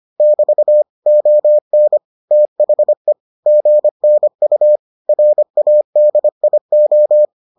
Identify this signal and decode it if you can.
Morse code, heard as a keyed tone, about 25 wpm, 600 Hz. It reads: <BT> ON THE GNU RADIO